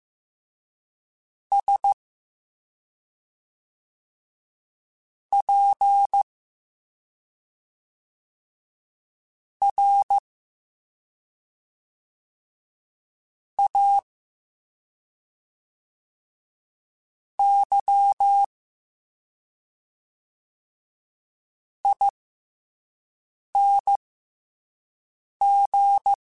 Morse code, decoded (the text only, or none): SPRAYING